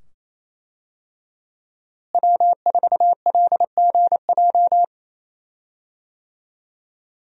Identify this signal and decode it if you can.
Morse code, heard as a keyed tone, about 27 wpm, 700 Hz. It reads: W4LGJ
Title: USA Call Signs - 5 Characters - Repeated 2x - 28wpm